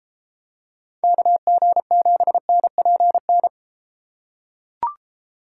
Morse code, heard as a keyed tone, about 33 wpm, 700 Hz. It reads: KG7NPN